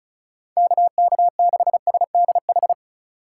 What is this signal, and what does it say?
Morse code, heard as a keyed tone, about 36 wpm, 700 Hz. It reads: KK6SDH
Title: USA Call Signs - 6 Characters - 8x Character Spacing - 35wpm